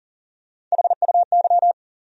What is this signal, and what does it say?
Morse code, heard as a keyed tone, about 40 wpm, 700 Hz. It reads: HUY
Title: USA Call Sign Postfixes - 3 Letters 40wpm